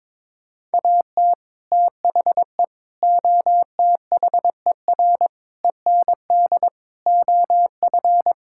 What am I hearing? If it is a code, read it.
Morse code, heard as a keyed tone, about 22 wpm, 700 Hz. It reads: AT THE OTHER END OF
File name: practice / 22wpm / Sets of 6 Words - 22wpm